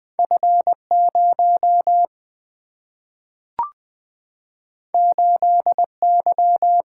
Morse code, heard as a keyed tone, about 20 wpm, 700 Hz. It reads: F0 8Y
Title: Single-Letter Single-Number 20wpm